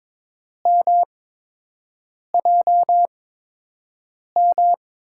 Morse code, heard as a keyed tone, about 22 wpm, spaced as for 6 wpm, 700 Hz. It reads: MJM
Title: USA Call Signs - 5 Characters - 8x Character Spacing - 22wpm